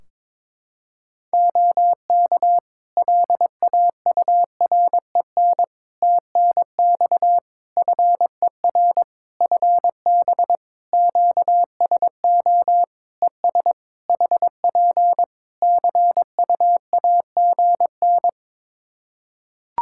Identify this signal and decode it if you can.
Morse code, heard as a keyed tone, about 21 wpm, 700 Hz. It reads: OK LAUREN TNX FER FB QSO ES HP CUAGN
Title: Practice QSO - Exchange 4 - Ending - 22wpm